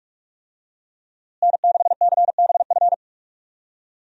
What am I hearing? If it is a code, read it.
Morse code, heard as a keyed tone, about 46 wpm, 700 Hz. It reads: N6CBF
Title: USA Call Signs - 5 Characters - Repeated 2x - 45wpm